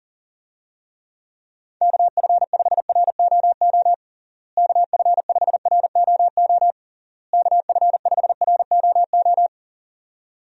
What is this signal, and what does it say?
Morse code, heard as a keyed tone, about 40 wpm, 700 Hz. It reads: KF5ROO KF5ROO KF5ROO
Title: USA Call Signs - 6 Characters - Repeated 3x - 40wpm